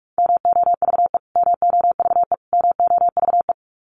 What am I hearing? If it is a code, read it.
Morse code, heard as a keyed tone, about 44 wpm, 700 Hz. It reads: MOVE MOVE MOVE
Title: Top 500 Words Repeated 45wpm